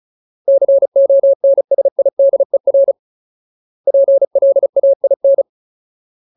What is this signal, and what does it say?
Morse code, heard as a keyed tone, about 34 wpm, 550 Hz. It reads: CONSIDER PLAIN
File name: Ultimate Word List / 35WPM Ultimate